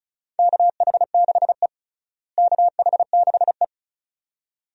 Morse code, heard as a keyed tone, about 36 wpm, 700 Hz. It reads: KH6E KH6E
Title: DXCC-100 Call Signs - 4 Characters - Repeated 2x - 35wpm